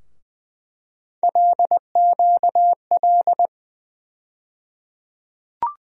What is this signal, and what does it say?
Morse code, heard as a keyed tone, about 20 wpm, 700 Hz. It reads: LQL
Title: USA Call Sign Postfixes - 3 Letters 20wpm